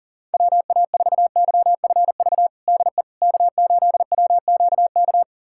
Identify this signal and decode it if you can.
Morse code, heard as a keyed tone, about 40 wpm, 700 Hz. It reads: WA4YFV DE K8WQK
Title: USA Callsign DE Callsign 40wpm